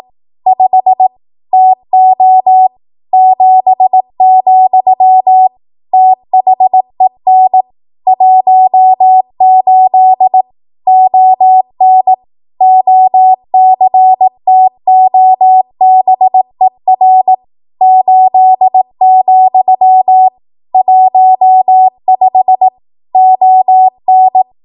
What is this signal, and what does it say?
Morse code, heard as a keyed tone, about 18 wpm, 750 Hz. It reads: 5 TO 7, THEN 18 ON OCTOBER 8, 15 ON
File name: Archive / 18 WPM / 18WPM